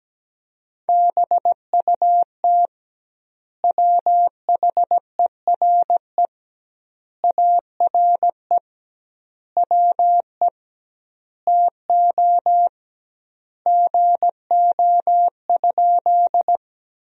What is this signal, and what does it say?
Morse code, heard as a keyed tone, about 17 wpm, 700 Hz. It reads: BUT WHERE ARE WE TO GO?
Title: 2x Word-Spacing - Sentences from Top 200 Words - 17wpm